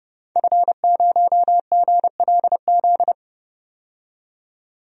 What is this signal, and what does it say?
Morse code, heard as a keyed tone, about 31 wpm, 700 Hz. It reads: F0GLZ
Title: DXCC-100 Call Signs - 5 Characters - 30wpm